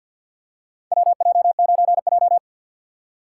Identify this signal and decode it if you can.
Morse code, heard as a keyed tone, about 47 wpm, 700 Hz. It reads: WJ9J
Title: USA Call Signs - 4 Characters - 2x Character Spacing - 50wpm